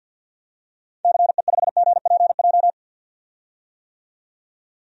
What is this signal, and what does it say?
Morse code, heard as a keyed tone, about 52 wpm, 700 Hz. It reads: CE5GPJ
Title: DXCC-100 Call Signs - 6 Characters - 50wpm